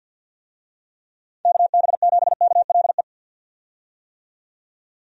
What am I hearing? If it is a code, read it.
Morse code, heard as a keyed tone, about 52 wpm, 700 Hz. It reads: KB7KLE
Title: USA Call Signs - 6 Characters - 50wpm